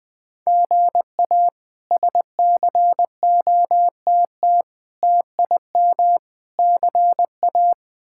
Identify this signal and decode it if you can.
Morse code, heard as a keyed tone, about 20 wpm, 700 Hz. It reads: GA SCOTT TIM CA
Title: K1USN SST QSO Practice - 20wpm